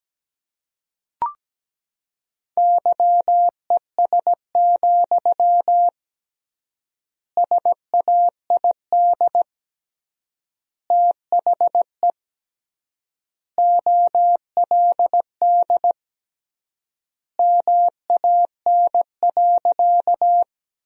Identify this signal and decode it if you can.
Morse code, heard as a keyed tone, about 17 wpm, 700 Hz. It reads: YES, SAID THE OLD MAN.
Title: 3x Word-Spacing - Sentences from Top 500 Words - 17wpm